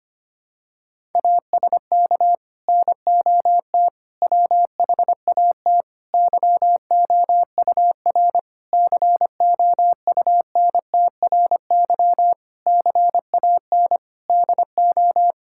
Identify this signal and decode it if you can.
Morse code, heard as a keyed tone, about 25 wpm, 700 Hz. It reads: ASK NOT WHAT YOUR COUNTRY CAN DO
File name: Sets of 9 Words - 25wpm